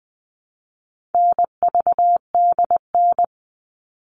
Morse code, heard as a keyed tone, about 20 wpm, 700 Hz. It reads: NVDN
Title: All Cut Numbers - 4-Digit 20wpm